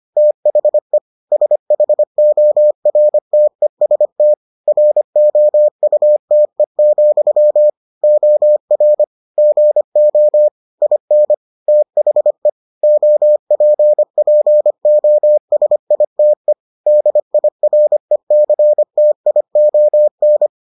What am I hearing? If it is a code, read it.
Morse code, heard as a keyed tone, about 25 wpm, 600 Hz. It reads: THE SHORTEST ROUTE, OR GO IN THE OPPOSITE DIRECTION